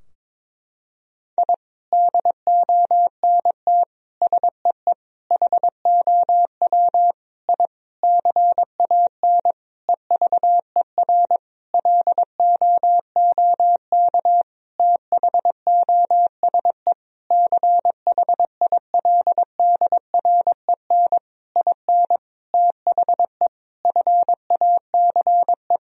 Morse code, heard as a keyed tone, about 22 wpm, 700 Hz. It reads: I DONT SEE HOW I CAN EVER LOOK THOSE CHILDREN IN THE FACE
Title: Sentences from Top 1000 Words - Review Part 2 - 22wpm